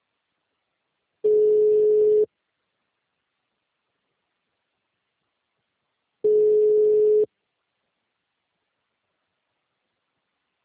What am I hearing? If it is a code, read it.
Ringback tone (Europe, 425 Hz).